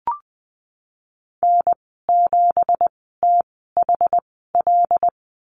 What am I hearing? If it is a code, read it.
Morse code, heard as a keyed tone, about 20 wpm, spaced as for 14 wpm, 700 Hz. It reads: N7THL
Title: 167 Lesson 27g - Number 7 - Call Signs 20wpm